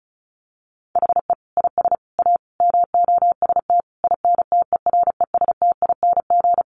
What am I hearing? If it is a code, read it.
Morse code, heard as a keyed tone, about 34 wpm, 700 Hz. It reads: HE IS A MOST INTERESTING
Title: Sentences from Top 1000 Words 35wpm